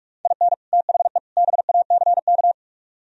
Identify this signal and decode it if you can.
Morse code, heard as a keyed tone, about 46 wpm, 700 Hz. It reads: IN THE BACK